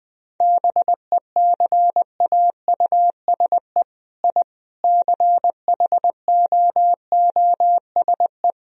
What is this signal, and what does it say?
Morse code, heard as a keyed tone, about 20 wpm, 700 Hz. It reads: BECAUSE I CHOOSE
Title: Sentences from Top 1000 Words - Review Part 4 - 20wpm